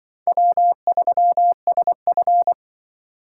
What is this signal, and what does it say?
Morse code, heard as a keyed tone, about 25 wpm, 700 Hz. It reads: W3SF
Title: Speed Racing - USA 1x2 Call Signs 20wpm